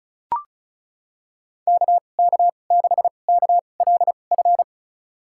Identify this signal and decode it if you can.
Morse code, heard as a keyed tone, about 34 wpm, spaced as for 25 wpm, 700 Hz. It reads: KK6KLF
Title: USA Call Signs - 6 Characters - 2x Character Spacing - 35wpm